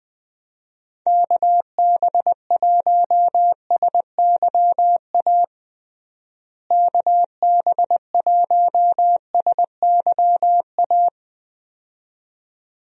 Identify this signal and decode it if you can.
Morse code, heard as a keyed tone, about 20 wpm, 700 Hz. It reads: KB1SYA KB1SYA
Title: USA Call Signs - 6 Characters - Repeated 2x - 20wpm